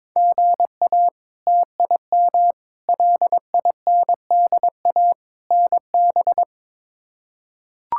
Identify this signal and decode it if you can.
Morse code, heard as a keyed tone, about 22 wpm, 700 Hz. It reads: GA TIM LINDA NB
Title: K1USN SST QSO Practice - 22wpm